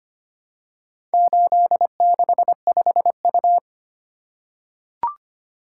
Morse code, heard as a keyed tone, about 25 wpm, 700 Hz. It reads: 865U
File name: practice / 25wpm / Four Letter-Number Combinations - 25wpm